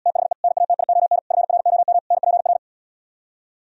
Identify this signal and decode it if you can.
Morse code, heard as a keyed tone, about 77 wpm, 700 Hz. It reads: THE NATION FROM NPR